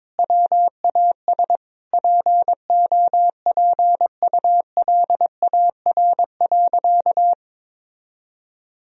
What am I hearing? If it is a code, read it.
Morse code, heard as a keyed tone, about 22 wpm, 700 Hz. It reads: WAS POPULAR.
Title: Sentences from Top 1000 Words - Review Part 3 - 22wpm